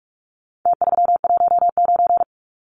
Morse code, heard as a keyed tone, about 44 wpm, 700 Hz. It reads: T319
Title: Numbers - Contest Style - 4-Digit 45wpm